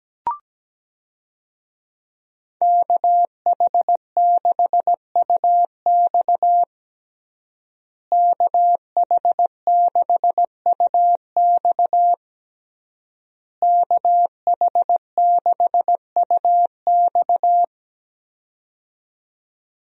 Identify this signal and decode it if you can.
Morse code, heard as a keyed tone, about 17 wpm, 700 Hz. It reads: KH6UX KH6UX KH6UX
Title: USA Call Signs - 5 Characters - Repeated 3x - 17wpm